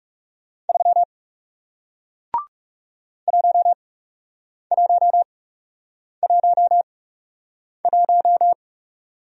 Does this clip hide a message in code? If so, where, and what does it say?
Yes: Morse code from 0.7 to 8.5 s, heard as a keyed tone, about 39 wpm, spaced as for 8 wpm, 700 Hz. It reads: 3 1111